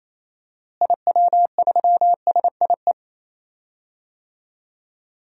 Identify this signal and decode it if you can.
Morse code, heard as a keyed tone, about 28 wpm, 700 Hz. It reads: IW3SIE
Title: DXCC-100 Call Signs - 6 Characters - Repeated 3x - 28wpm